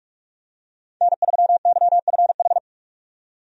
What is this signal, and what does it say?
Morse code, heard as a keyed tone, about 46 wpm, 700 Hz. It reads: N3YFH
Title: USA Call Signs - 5 Characters - 45wpm